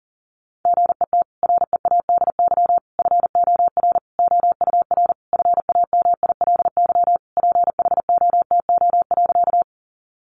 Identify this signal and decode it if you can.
Morse code, heard as a keyed tone, about 40 wpm, 700 Hz. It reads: GET READY FOR OUR FAMILY PHOTO.
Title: Sentences from Top 1200 Words 40wpm